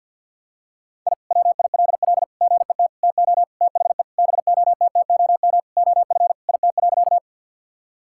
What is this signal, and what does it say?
Morse code, heard as a keyed tone, about 51 wpm, 700 Hz. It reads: I WILL GET TO THE BOTTOM OF IT.
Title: Sentences from Top 1000 Words - Review Part 6 - 50wpm